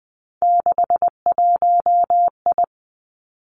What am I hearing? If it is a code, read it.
Morse code, heard as a keyed tone, about 20 wpm, 700 Hz. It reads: 61I
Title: Three Challenging Letter-Number Combinations 20wpm